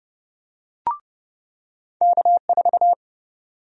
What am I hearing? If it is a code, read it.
Morse code, heard as a keyed tone, about 30 wpm, 700 Hz. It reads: K4
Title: USA Call Sign Prefixes - 30wpm